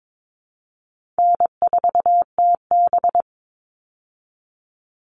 Morse code, heard as a keyed tone, about 22 wpm, 700 Hz. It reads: N4TB
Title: Easy Call Signs Repeated 3x - 22wpm